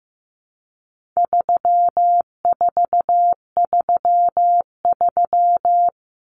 Morse code, heard as a keyed tone, about 15 wpm, 700 Hz. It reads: 3433
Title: Numbers - 4-Digit 15wpm